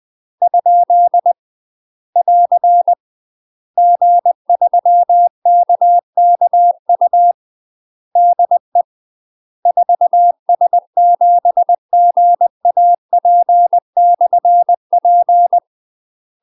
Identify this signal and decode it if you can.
Morse code, heard as a keyed tone, about 20 wpm, 700 Hz. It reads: ? <AR> G3KKU DE 4S7GAP/P
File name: UK MASTER QSOs TRAINING / 20WPM / UK MASTER QSOs 0013